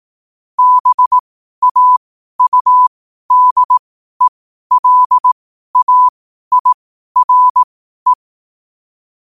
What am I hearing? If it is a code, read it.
Morse code, heard as a keyed tone, about 18 wpm, spaced as for 12 wpm, 1000 Hz. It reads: BAUDELAIRE
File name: qotdfiles / 12wpm